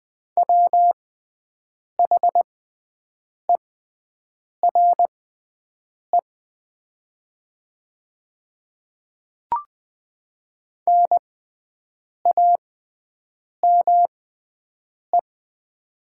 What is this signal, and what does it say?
Morse code, heard as a keyed tone, about 20 wpm, spaced as for 7 wpm, 700 Hz. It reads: WHERE NAME
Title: 6x Character Spacing - Top 200 Words - 20wpm